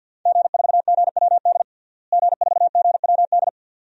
Morse code, heard as a keyed tone, about 50 wpm, 700 Hz. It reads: G4GWD G4GWD
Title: DXCC-100 Call Signs - 5 Characters - Repeated 2x - 50wpm